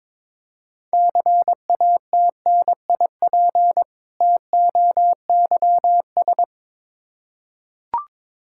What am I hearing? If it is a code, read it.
Morse code, heard as a keyed tone, about 22 wpm, 700 Hz. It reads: CATNIP TOYS
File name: Difficult Words with Hints - 22wpm